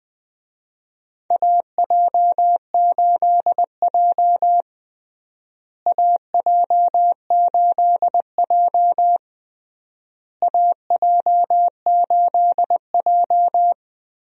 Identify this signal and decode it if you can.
Morse code, heard as a keyed tone, about 21 wpm, 700 Hz. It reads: AJ8J AJ8J AJ8J
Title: USA Call Signs - 4 Characters - Repeated 3x - 20wpm